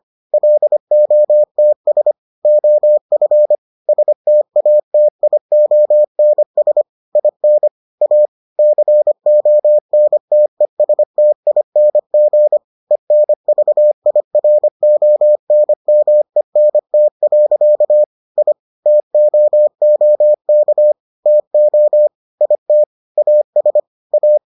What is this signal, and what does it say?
Morse code, heard as a keyed tone, about 25 wpm, 600 Hz. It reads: LOTS OF STATIONS IN A CONTESTING ENVIRONMENT. I TOOK TO IT AS A